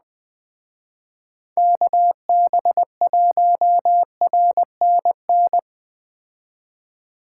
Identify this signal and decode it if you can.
Morse code, heard as a keyed tone, about 21 wpm, 700 Hz. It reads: KB1RNN